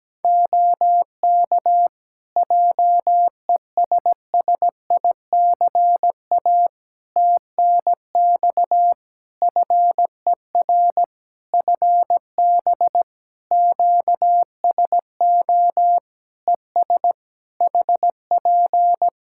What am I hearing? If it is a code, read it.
Morse code, heard as a keyed tone, about 17 wpm, 700 Hz. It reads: OK JESSICA TNX FER FB QSO ES HP
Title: Practice QSO - Exchange 4 - Ending - 17wpm